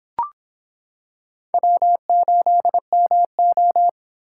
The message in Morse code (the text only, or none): W8MO